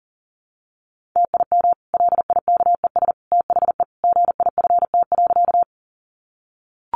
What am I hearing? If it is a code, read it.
Morse code, heard as a keyed tone, about 40 wpm, 700 Hz. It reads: TIM LIKES THE GIFT.
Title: Easy Phrases 40wpm